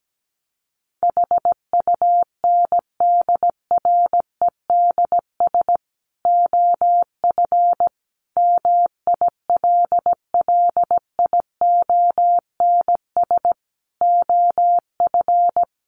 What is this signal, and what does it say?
Morse code, heard as a keyed tone, about 17 wpm, 700 Hz. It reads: HUNDREDS OF MILLIONS OF
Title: Sets of 5 Words 17wpm